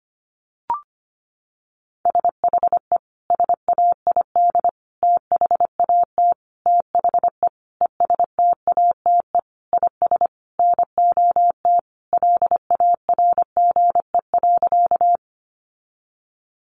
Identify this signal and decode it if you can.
Morse code, heard as a keyed tone, about 25 wpm, 700 Hz. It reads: SHE SAID THAT THE ESTATE IS NOT LARGE.
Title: Sentences from Top 1200 Words 25wpm